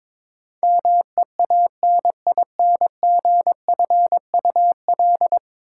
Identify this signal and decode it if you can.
Morse code, heard as a keyed tone, about 22 wpm, 700 Hz. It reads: MEANINGFUL
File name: practice / 22wpm / 10 Letter Words - First 3 Letters Given - 22wpm